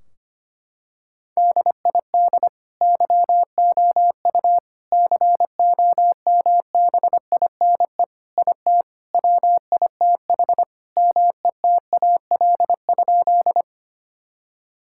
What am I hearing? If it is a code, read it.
Morse code, heard as a keyed tone, about 25 wpm, 700 Hz. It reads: DID YOU COMBINE IT WITH METAL?
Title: Sentences from Top 1600 Words 25wpm